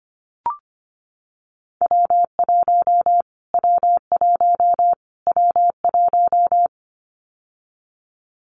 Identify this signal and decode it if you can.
Morse code, heard as a keyed tone, about 25 wpm, 700 Hz. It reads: W1 W1 W1
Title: Callsign Prefixes for Top 25 DXCC - Repeated 3x - 25wpm